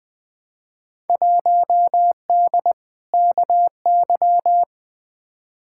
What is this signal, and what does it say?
Morse code, heard as a keyed tone, about 20 wpm, 700 Hz. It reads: 1D KY
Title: Field Day Exchanges - 20wpm